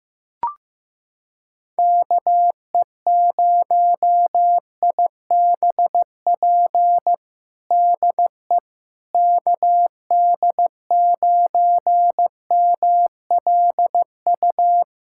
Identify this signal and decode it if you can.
Morse code, heard as a keyed tone, about 15 wpm, 700 Hz. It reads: KE0IBP DE KD9MLU
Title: USA Callsign DE Callsign - 6 Characters - 15wpm